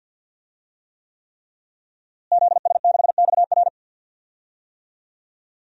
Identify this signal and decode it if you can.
Morse code, heard as a keyed tone, about 52 wpm, 700 Hz. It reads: ZS6XR
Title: DXCC-100 Call Signs - 5 Characters - Repeated 3x - 50wpm